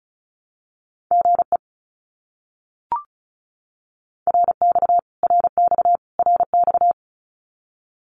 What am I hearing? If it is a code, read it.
Morse code, heard as a keyed tone, about 34 wpm, 700 Hz. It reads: GE RX RX RX
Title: Abbreviations Repeated 3x - 35wpm